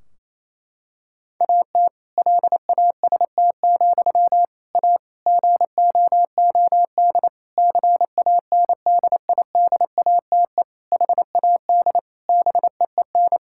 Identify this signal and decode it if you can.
Morse code, heard as a keyed tone, about 28 wpm, 700 Hz. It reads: AT LAST, A GOOD CANDIDATE HAD BEEN
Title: Sentences from Top 1600 Words 28wpm